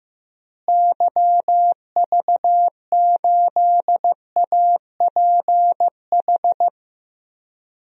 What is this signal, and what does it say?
Morse code, heard as a keyed tone, about 15 wpm, 700 Hz. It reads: YV8APH
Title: DXCC-100 Call Signs - 6 Characters - Repeated 3x - 15wpm